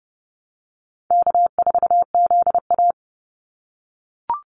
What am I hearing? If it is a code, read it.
Morse code, heard as a keyed tone, about 32 wpm, 700 Hz. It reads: K4ZA
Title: Easy Call Signs - 30wpm